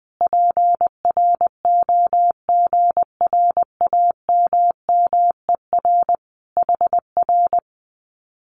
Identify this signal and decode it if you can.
Morse code, heard as a keyed tone, about 20 wpm, 700 Hz. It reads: PROGRAMMER HR
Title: Common QSO Phrases Encore 20wpm